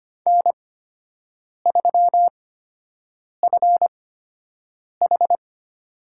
Morse code, heard as a keyed tone, about 25 wpm, spaced as for 7 wpm, 700 Hz. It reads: N3FH
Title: USA Call Signs - 5 Characters - 8x Character Spacing - 25wpm